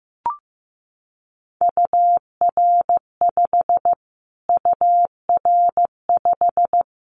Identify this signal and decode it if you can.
Morse code, heard as a keyed tone, about 14 wpm, 700 Hz. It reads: UR5 UR5
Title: Callsign Prefixes for Top 25 DXCC - Repeated 3x - 15wpm